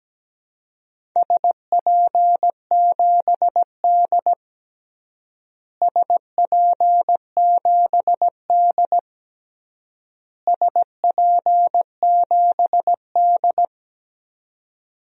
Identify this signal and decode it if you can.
Morse code, heard as a keyed tone, about 17 wpm, 700 Hz. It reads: SP7D SP7D SP7D
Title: DXCC-100 Call Signs - 4 Characters - Repeated 3x - 17wpm